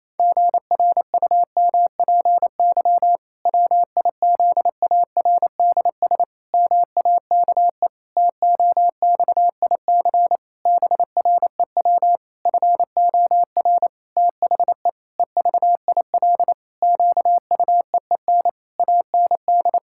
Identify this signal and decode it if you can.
Morse code, heard as a keyed tone, about 28 wpm, 700 Hz. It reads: GRUMPY WIZARDS MAKE TOXIC BREW FOR THE EVIL QUEEN AND
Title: Pangram Challenge v2 - 28wpm